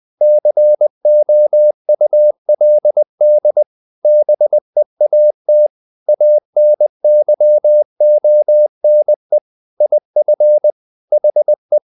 Morse code, heard as a keyed tone, about 20 wpm, 600 Hz. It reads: COULD BEAT ANYONE IF HE